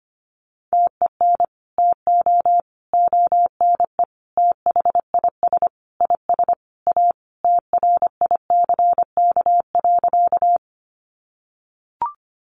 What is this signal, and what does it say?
Morse code, heard as a keyed tone, about 25 wpm, 700 Hz. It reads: TEN TO ONE THIS IS A TRICK.
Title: Sentences from Top 1500 Words 25wpm